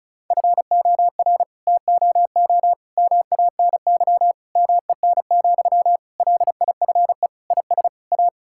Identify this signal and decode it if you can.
Morse code, heard as a keyed tone, about 36 wpm, 700 Hz. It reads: FOR TOO MANY MEN, LIFE IS A
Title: Sentences from Top 1000 Words - Review Part 6 - 35wpm